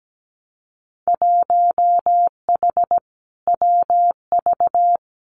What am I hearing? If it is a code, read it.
Morse code, heard as a keyed tone, about 17 wpm, 700 Hz. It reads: 1H WV
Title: Winter Field Day Exchanges Encore 17wpm